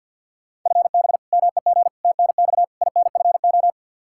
Morse code, heard as a keyed tone, about 51 wpm, 700 Hz. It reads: FB MEG TNX INFO
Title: Practice QSO - Exchange 3 - More Info - 50wpm